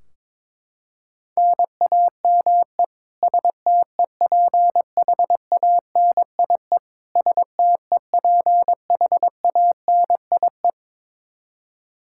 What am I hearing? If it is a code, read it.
Morse code, heard as a keyed tone, about 22 wpm, 700 Hz. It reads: NAME STEPHANIE STEPHANIE
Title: Practice QSO - Exchange 1 - Beginning - 22wpm